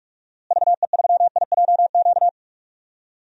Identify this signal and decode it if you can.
Morse code, heard as a keyed tone, about 46 wpm, 700 Hz. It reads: VE3IJQ
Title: DXCC-100 Call Signs - 6 Characters - Repeated 2x - 45wpm